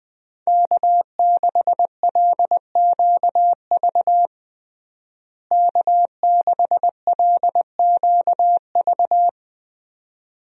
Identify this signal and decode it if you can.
Morse code, heard as a keyed tone, about 20 wpm, 700 Hz. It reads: K6LQV K6LQV
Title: USA Call Signs - 5 Characters - Repeated 2x - 20wpm